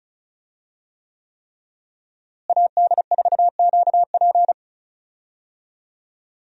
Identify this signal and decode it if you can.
Morse code, heard as a keyed tone, about 36 wpm, 700 Hz. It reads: AD4QP
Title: USA Call Signs - 5 Characters - Repeated 3x - 35wpm